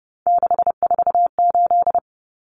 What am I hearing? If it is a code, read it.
Morse code, heard as a keyed tone, about 28 wpm, 700 Hz. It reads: 648
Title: Numbers - 3-Digit 30wpm